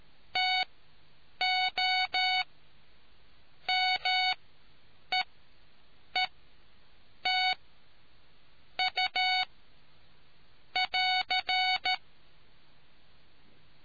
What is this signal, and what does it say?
Morse code, heard as a keyed tone, about 13 wpm, spaced as for 7 wpm, 740 Hz. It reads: TO MEET U <AR>